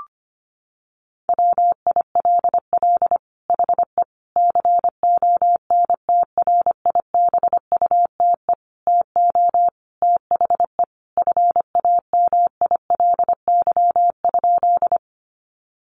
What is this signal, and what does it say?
Morse code, heard as a keyed tone, about 25 wpm, 700 Hz. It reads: WILL HE CONTRIBUTE TO THE FAMILY?
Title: Sentences from Top 1400 Words 25wpm